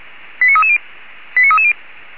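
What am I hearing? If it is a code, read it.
Five-tone selective call (ZVEI), heard as quick runs of five tones, tones of about 70 ms. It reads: 89330 89330
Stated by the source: Received by SDR, ZVEI five-tone selective call 89315, fire brigade Feuerwehr Orlamünde.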